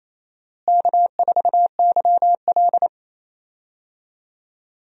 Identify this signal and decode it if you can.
Morse code, heard as a keyed tone, about 28 wpm, 700 Hz. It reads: K4YL
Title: USA Call Signs - 4 Characters - Repeated 2x - 28wpm